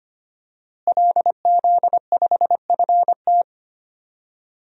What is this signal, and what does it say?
Morse code, heard as a keyed tone, about 25 wpm, 700 Hz. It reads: LZ5FT
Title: DXCC-100 Call Signs - 5 Characters - Repeated 2x - 25wpm